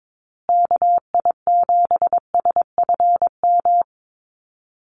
Morse code, heard as a keyed tone, about 22 wpm, 700 Hz. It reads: KI7SFM
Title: USA Call Signs Repeated 3x 22wpm